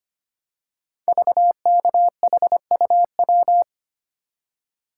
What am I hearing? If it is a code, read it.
Morse code, heard as a keyed tone, about 25 wpm, 700 Hz. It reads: VKHUW
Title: Code Groups - 5 Letters - 25wpm